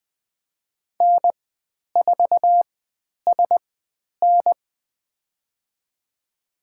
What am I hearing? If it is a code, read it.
Morse code, heard as a keyed tone, about 20 wpm, spaced as for 10 wpm, 700 Hz. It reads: N4SN
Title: 109 Lesson 18g - Letter m - Call Signs 20wpm